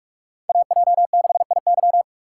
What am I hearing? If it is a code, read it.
Morse code, heard as a keyed tone, about 46 wpm, 700 Hz. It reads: AJ6IY
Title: USA Call Signs - 5 Characters - 2x Character Spacing - 45wpm